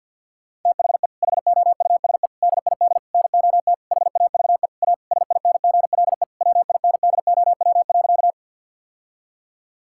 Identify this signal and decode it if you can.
Morse code, heard as a keyed tone, about 50 wpm, 700 Hz. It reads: THE HOUSE DID NOT HAVE A SINGLE WINDOW.
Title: Sentences from Top 1600 Words 50wpm